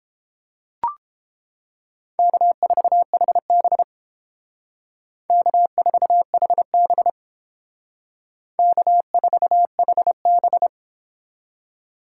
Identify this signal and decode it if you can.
Morse code, heard as a keyed tone, about 28 wpm, 700 Hz. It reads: K4HB K4HB K4HB